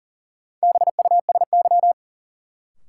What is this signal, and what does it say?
Morse code, heard as a keyed tone, about 40 wpm, 700 Hz. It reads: BUSY
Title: Sentences from Top 1000 Words - Review Part 2 - New Word Drill - 40wpm